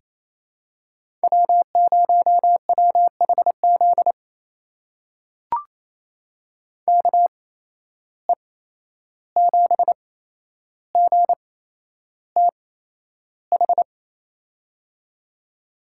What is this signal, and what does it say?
Morse code, heard as a keyed tone, about 28 wpm, spaced as for 11 wpm, 700 Hz. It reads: W0WHZ KE7GTH
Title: USA Call Signs - 8x Character Spacing - 28wpm